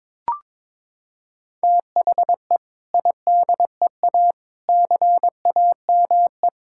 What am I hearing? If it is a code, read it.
Morse code, heard as a keyed tone, about 22 wpm, 700 Hz. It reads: THE IDEA CAME
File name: practice / 22wpm / Sentences with 2-Word Phrases - 22wpm